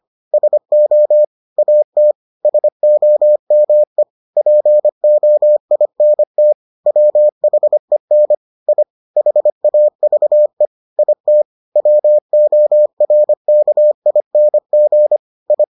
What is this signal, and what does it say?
Morse code, heard as a keyed tone, about 25 wpm, 600 Hz. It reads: SO AT SOME POINT WHEN I HAVE IT WORKING I